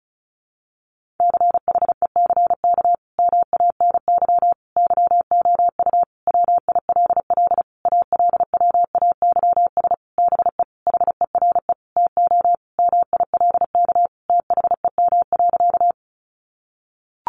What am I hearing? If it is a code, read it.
Morse code, heard as a keyed tone, about 34 wpm, 700 Hz. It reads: CHECK MANY YOU WILL ALWAYS BE HERE TO MILK THEM.
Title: Sentences from Top 1500 Words 35wpm